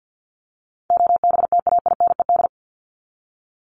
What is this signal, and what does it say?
Morse code, heard as a keyed tone, about 50 wpm, 700 Hz. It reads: OBTAINED
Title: ED Suffix - Top 100 Words with Base Words 50wpm